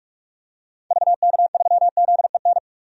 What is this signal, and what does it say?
Morse code, heard as a keyed tone, about 46 wpm, 700 Hz. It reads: VK3ZEN
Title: DXCC-100 Call Signs - 6 Characters - 45wpm